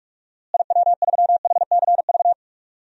Morse code, heard as a keyed tone, about 46 wpm, 700 Hz. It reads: IW3HCV
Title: DXCC-100 Call Signs - 6 Characters - Repeated 3x - 45wpm